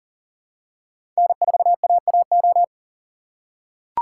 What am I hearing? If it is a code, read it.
Morse code, heard as a keyed tone, about 37 wpm, 700 Hz. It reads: N4AAO
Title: USA Call Signs - 2x Character Spacing - 40wpm